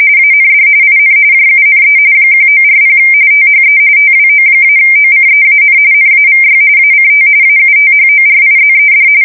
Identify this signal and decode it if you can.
RTTY, about 45.45 baud. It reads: AND IN JUNE 1940 THE TWO-OCEAN NAVY ACT AUTHORIZED
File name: fsk170 / usnavy4